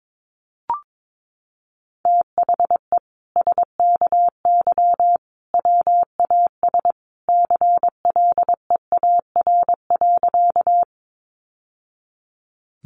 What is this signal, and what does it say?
Morse code, heard as a keyed tone, about 22 wpm, 700 Hz. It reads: THE SKY WAS CLEAR.
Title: Sentences from Top 1200 Words 22wpm